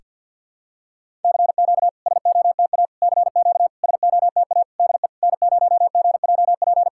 Morse code, heard as a keyed tone, about 52 wpm, 690 Hz. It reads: CQ SOTA CQ SOTA DE N0GJP
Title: SOTA QSO Practice 50wpm